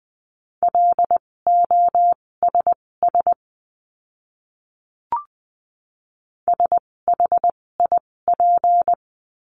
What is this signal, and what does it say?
Morse code, heard as a keyed tone, about 20 wpm, spaced as for 16 wpm, 700 Hz. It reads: LOSS SHIP
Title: 256 Lesson 42a - Words 12-20wpm